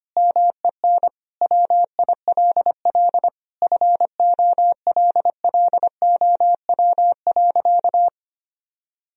Morse code, heard as a keyed tone, about 25 wpm, 700 Hz. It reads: MEN WILL FOLLOW.
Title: Sentences from Top 1000 Words - Review Part 1 - 25wpm